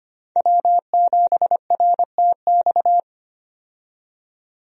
Morse code, heard as a keyed tone, about 25 wpm, 700 Hz. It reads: W7RTX
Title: USA Call Signs - 5 Characters - 25wpm